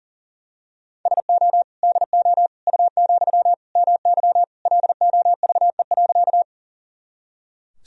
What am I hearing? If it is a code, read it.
Morse code, heard as a keyed tone, about 40 wpm, 700 Hz. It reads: SO DO U, MY LOVE.
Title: Sentences from Top 400 Words and Abbreviations 40wpm